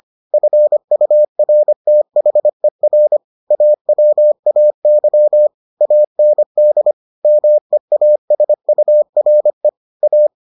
Morse code, heard as a keyed tone, about 25 wpm, 600 Hz. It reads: FURTHER AWAY AND MEASURE A